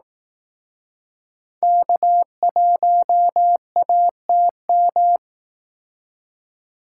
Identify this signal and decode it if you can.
Morse code, heard as a keyed tone, about 19 wpm, 700 Hz. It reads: K1ATM
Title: Speed Racing - USA 1x3 Call Signs 15wpm